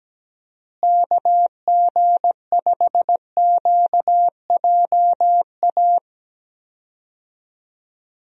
Morse code, heard as a keyed tone, about 17 wpm, 700 Hz. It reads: KG5QJA
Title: USA Call Signs - 6 Characters - Repeated 3x - 17wpm